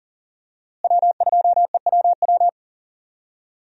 Morse code, heard as a keyed tone, about 39 wpm, 700 Hz. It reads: W2EWW
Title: USA Call Signs - 5 Characters - Repeated 2x - 40wpm